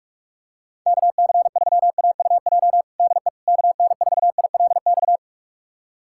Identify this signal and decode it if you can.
Morse code, heard as a keyed tone, about 41 wpm, 690 Hz. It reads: KC3AUJ DE KN4ILX
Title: USA Callsign DE Callsign - 6 Characters - 45wpm